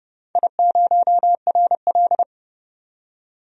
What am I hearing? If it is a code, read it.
Morse code, heard as a keyed tone, about 30 wpm, 700 Hz. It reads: I0RL
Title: DXCC-100 Call Signs - 4 Characters - 30wpm